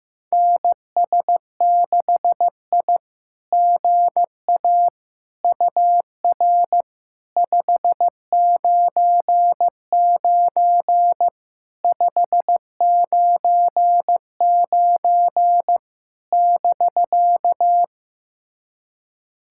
Morse code, heard as a keyed tone, about 15 wpm, 700 Hz. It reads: NS6I GA UR 599 599 <BK>
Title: SOTA QSO Practice 15wpm